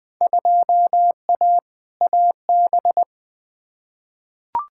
Morse code, heard as a keyed tone, about 20 wpm, 700 Hz. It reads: 2A AB
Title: Field Day Exchanges - 20wpm